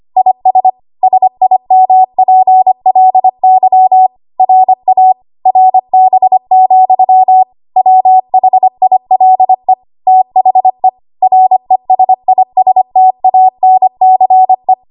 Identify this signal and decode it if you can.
Morse code, heard as a keyed tone, about 25 wpm, 750 Hz. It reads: IS SIMPLY RA RB, WHILE THE RESISTANCE
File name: Archive / 25 WPM / 25WPM